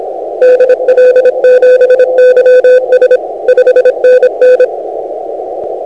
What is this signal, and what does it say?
Morse code, heard as a keyed tone, about 26 wpm, 500 Hz. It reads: DL7YS 5NN